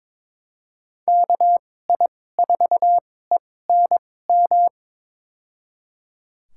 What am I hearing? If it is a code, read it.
Morse code, heard as a keyed tone, about 21 wpm, spaced as for 16 wpm, 700 Hz. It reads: KI4ENM
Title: USA Call Signs - 6 Characters - 2x Character Spacing - 22wpm